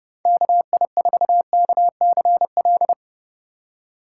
Morse code, heard as a keyed tone, about 30 wpm, 700 Hz. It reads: KI4KCL